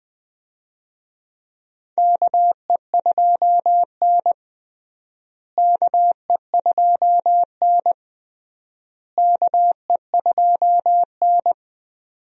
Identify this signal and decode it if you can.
Morse code, heard as a keyed tone, about 20 wpm, 700 Hz. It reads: KE2N KE2N KE2N
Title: USA Call Signs - 4 Characters - Repeated 3x - 20wpm